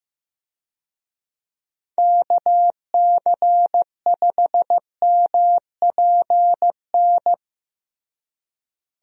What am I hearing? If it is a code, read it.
Morse code, heard as a keyed tone, about 16 wpm, 700 Hz. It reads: KC5MPN
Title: USA Call Signs - 6 Characters - Repeated 2x - 15wpm